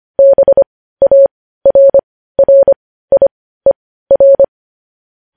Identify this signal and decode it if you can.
Morse code, heard as a keyed tone, about 25 wpm, spaced as for 15 wpm, 550 Hz. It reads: BARRIER
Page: News Headlines in Morse Code at 15 WPM